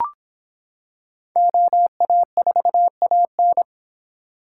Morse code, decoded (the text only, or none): OA4AN